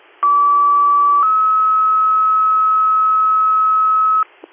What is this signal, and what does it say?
Two-tone page, heard as a one-second tone, then a three-second tone, about 1150 then 1290 Hz. It is heard from 0.2 to 4.2 s.